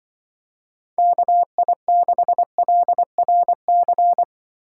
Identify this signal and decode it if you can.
Morse code, heard as a keyed tone, about 24 wpm, 700 Hz. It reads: KI6LRC
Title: Speed Racing - USA Call Signs 20wpm